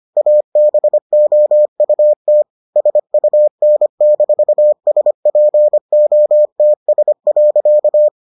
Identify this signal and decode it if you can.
Morse code, heard as a keyed tone, about 25 wpm, 600 Hz. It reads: ABOUT SUN-SPOTS.